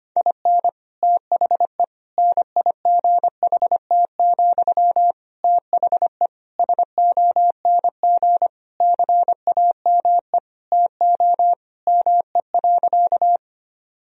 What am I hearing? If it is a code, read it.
Morse code, heard as a keyed tone, about 25 wpm, 700 Hz. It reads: IN THE NIGHT, THE SONG CAME TO ME.
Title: Sentences from Top 1000 Words - Review Part 2 - 25wpm